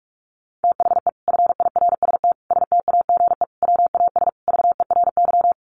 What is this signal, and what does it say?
Morse code, heard as a keyed tone, about 44 wpm, 700 Hz. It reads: THE FIRST STAGE WAS VERY
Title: Sentences from Top 1200 Words 45wpm